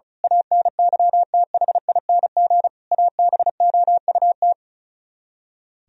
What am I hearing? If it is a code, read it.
Morse code, heard as a keyed tone, about 32 wpm, 700 Hz. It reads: ANYTHING ABOUT
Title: Speed-Racing - Sets of 5 Words 35wpm